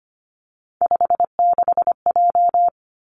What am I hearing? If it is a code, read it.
Morse code, heard as a keyed tone, about 25 wpm, 700 Hz. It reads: 56J
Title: Three Challenging Letter-Number Combinations 25wpm